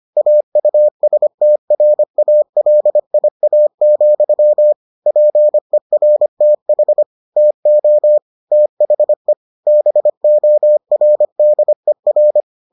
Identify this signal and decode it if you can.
Morse code, heard as a keyed tone, about 25 wpm, 600 Hz. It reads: AUSTRALIA, PERTH TO THE BORDER